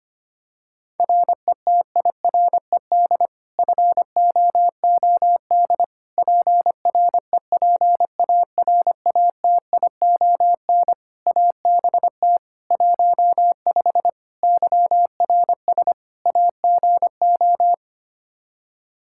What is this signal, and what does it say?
Morse code, heard as a keyed tone, about 25 wpm, 700 Hz. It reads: RETIRED FOOD PREPARATION ABT 15 YRS AGO
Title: K1USN SST QSO Practice 25wpm